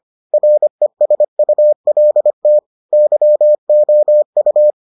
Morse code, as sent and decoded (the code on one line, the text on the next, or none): .-. ... ..- .-.. - / -.-- --- ..-
RESULT YOU